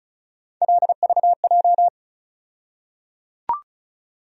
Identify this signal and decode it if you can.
Morse code, heard as a keyed tone, about 36 wpm, 700 Hz. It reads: LVJ
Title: USA Call Sign Postfixes - 3 Letters 35wpm